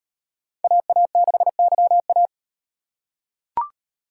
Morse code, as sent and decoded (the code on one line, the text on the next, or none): .- .- -.... -.-- .-
AA6YA